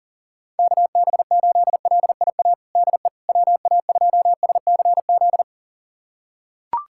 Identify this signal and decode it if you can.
Morse code, heard as a keyed tone, about 37 wpm, 700 Hz. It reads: KB8LIA DE WA2SCZ
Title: USA Callsign DE Callsign - 6 Characters - 40wpm